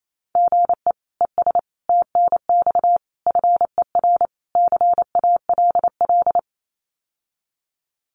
Morse code, heard as a keyed tone, about 28 wpm, 700 Hz. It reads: GE ES TNX FER CALL
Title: Common QSO Phrases Encore 28wpm